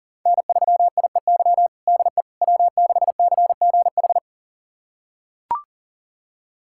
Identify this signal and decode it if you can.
Morse code, heard as a keyed tone, about 40 wpm, 700 Hz. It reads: N3IEY DE W6CGH
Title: USA Callsign DE Callsign - 5 Characters - 40wpm